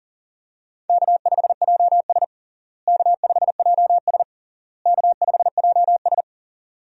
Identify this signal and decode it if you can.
Morse code, heard as a keyed tone, about 40 wpm, 700 Hz. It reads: K5JS K5JS K5JS
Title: USA Call Signs - 4 Characters - Repeated 3x - 40wpm